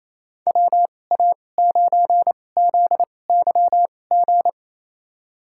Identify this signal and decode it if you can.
Morse code, heard as a keyed tone, about 28 wpm, spaced as for 20 wpm, 700 Hz. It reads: WA9ZYG
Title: USA Call Signs - 2x Character Spacing - 28wpm